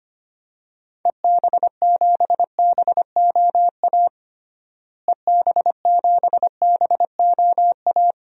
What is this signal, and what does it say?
Morse code, heard as a keyed tone, about 25 wpm, 700 Hz. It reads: EB7BOA EB7BOA
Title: DXCC-100 Call Signs - 6 Characters - Repeated 3x - 25wpm